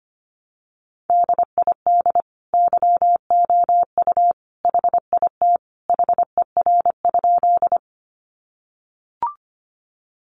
Morse code, heard as a keyed tone, about 25 wpm, 700 Hz. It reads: DID YOU HIT HER?
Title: Sentences from Top 1200 Words 25wpm